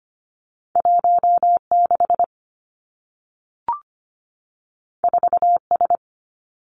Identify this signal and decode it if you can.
Morse code, heard as a keyed tone, about 25 wpm, 700 Hz. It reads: 16 4S
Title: Two Challenging Letter-Number Combinations 25wpm